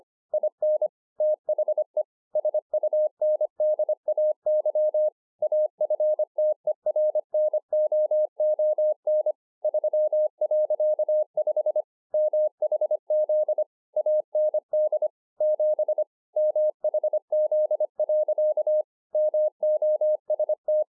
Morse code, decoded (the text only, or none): IN THE SUNDAY AFTERNOON 3.5 MHZ AND 7 MHZ. MOST